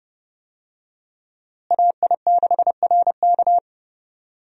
Morse code, heard as a keyed tone, about 30 wpm, 700 Hz. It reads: AI6RK
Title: USA Call Signs - 5 Characters - Repeated 3x - 30wpm